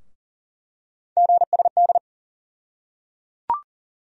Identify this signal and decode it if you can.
Morse code, heard as a keyed tone, about 38 wpm, 700 Hz. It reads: GSD
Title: USA Call Sign Postfixes - 3 Letters 40wpm